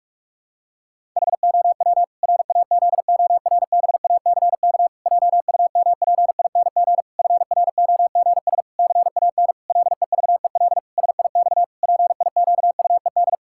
Morse code, heard as a keyed tone, about 46 wpm, 700 Hz. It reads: HOW RAZORBACK JUMPING FROGS CAN LEVEL SIX PIQUED
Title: Pangram Challenge v2 - 45wpm